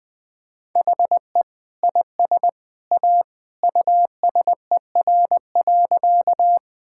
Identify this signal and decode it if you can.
Morse code, heard as a keyed tone, about 20 wpm, 700 Hz. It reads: HE IS A USER.
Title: Sentences from Top 1000 Words - Review Part 5 - 20wpm